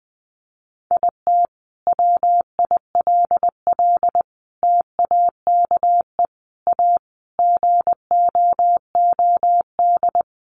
Morse code, heard as a keyed tone, about 20 wpm, 700 Hz. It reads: IT WILL TAKE A GOOD
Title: Sentences from Top 1300 Words 20wpm